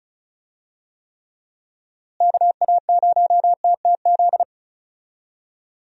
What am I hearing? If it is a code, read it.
Morse code, heard as a keyed tone, about 35 wpm, 700 Hz. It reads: KA0TTZ
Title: USA Call Signs - 6 Characters - Repeated 2x - 35wpm